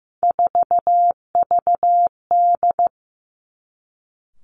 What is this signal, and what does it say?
Morse code, heard as a keyed tone, about 15 wpm, 700 Hz. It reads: 4VD
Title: Three Challenging Letter-Number Combinations 15wpm